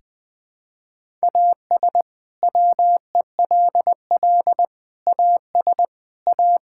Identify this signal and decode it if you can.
Morse code, heard as a keyed tone, about 20 wpm, 700 Hz. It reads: AS WELL AS A
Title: Sets of 6 Words - 20wpm